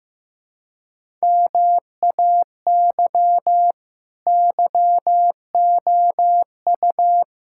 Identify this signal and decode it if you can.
Morse code, heard as a keyed tone, about 15 wpm, 700 Hz. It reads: MAY YOU
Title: Sentences from Top 1000 Words - Review Part 6 - 15wpm